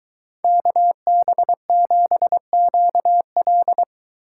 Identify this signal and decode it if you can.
Morse code, heard as a keyed tone, about 22 wpm, 700 Hz. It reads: KB7QL